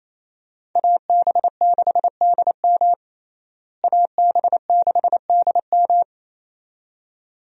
Morse code, heard as a keyed tone, about 28 wpm, 700 Hz. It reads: AB6DM AB6DM
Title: USA Call Signs - 5 Characters - Repeated 2x - 28wpm